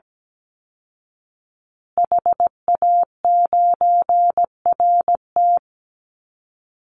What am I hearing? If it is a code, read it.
Morse code, heard as a keyed tone, about 17 wpm, 700 Hz. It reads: HA9RT
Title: Call Signs Repeated 2x - 17wpm